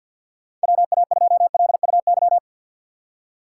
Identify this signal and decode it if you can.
Morse code, heard as a keyed tone, about 51 wpm, 700 Hz. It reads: WA2LUY
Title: USA Call Signs - 6 Characters - 4x Character Spacing - 50wpm